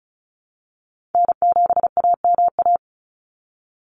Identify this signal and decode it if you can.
Morse code, heard as a keyed tone, about 34 wpm, 700 Hz. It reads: N7AMA
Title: NA SOTA Chaser Callsigns 35wpm